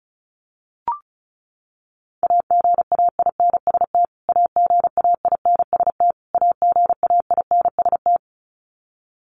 Morse code, heard as a keyed tone, about 34 wpm, 700 Hz. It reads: AGAINST AGAINST AGAINST
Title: Top 500 Words Repeated 35wpm